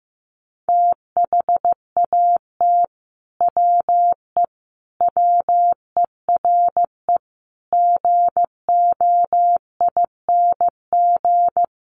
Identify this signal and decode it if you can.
Morse code, heard as a keyed tone, about 15 wpm, 700 Hz. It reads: THAT WE WERE GOING
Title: Sets of 5 Words Encore2 15wpm